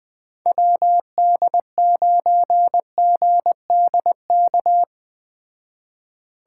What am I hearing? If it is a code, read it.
Morse code, heard as a keyed tone, about 20 wpm, 700 Hz. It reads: WD9GDK
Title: Speed Racing - USA 2x3 Call Signs 17wpm